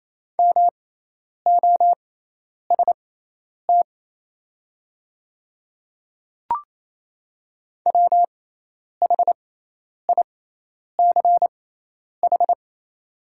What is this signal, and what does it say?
Morse code, heard as a keyed tone, about 28 wpm, spaced as for 10 wpm, 700 Hz. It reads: MOST WHICH
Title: 6x Character Spacing - Top 300 Words - 28wpm